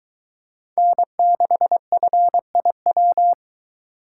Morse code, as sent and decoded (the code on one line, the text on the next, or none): -. -.... ..-. .. .--
N6FIW